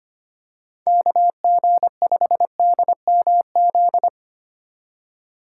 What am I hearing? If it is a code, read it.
Morse code, heard as a keyed tone, about 25 wpm, 700 Hz. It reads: KG5DMZ
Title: USA Call Signs - 6 Characters - 3x Character Spacing - 25wpm